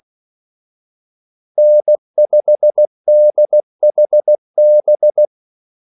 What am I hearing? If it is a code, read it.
Morse code, heard as a keyed tone, about 16 wpm, 600 Hz. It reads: N5DHB
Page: Kaldesignaler 16 wpm